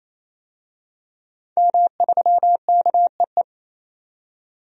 Morse code, heard as a keyed tone, about 28 wpm, 700 Hz. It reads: M3KEE
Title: DXCC-100 Call Signs - 5 Characters - 28wpm